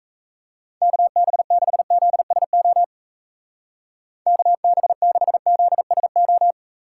Morse code, heard as a keyed tone, about 34 wpm, 700 Hz. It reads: KB6ZSO KB6ZSO
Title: Speed Racing - USA 2x3 Call Signs 28wpm